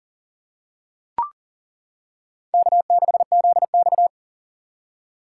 Morse code, heard as a keyed tone, about 40 wpm, 700 Hz. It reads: K6ZX
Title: K1USN SST QSO Practice - 40wpm